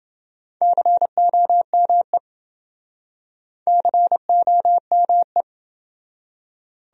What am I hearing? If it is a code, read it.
Morse code, heard as a keyed tone, about 30 wpm, 700 Hz. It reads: COME COME